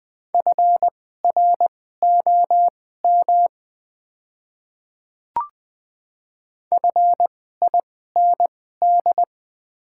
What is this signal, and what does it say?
Morse code, heard as a keyed tone, about 20 wpm, spaced as for 14 wpm, 700 Hz. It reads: FROM FIND
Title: 2x Character Spacing - Top 100 Words - 20wpm